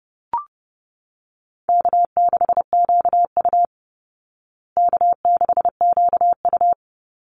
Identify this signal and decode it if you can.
Morse code, heard as a keyed tone, about 30 wpm, 700 Hz. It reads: K6QU K6QU
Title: Easy Call Signs Repeated 3x - 30wpm